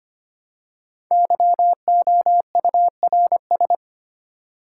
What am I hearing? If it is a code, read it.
Morse code, heard as a keyed tone, about 25 wpm, 700 Hz. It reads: YOURS
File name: 5 Letter Words - First 3 Letters Given - 25wpm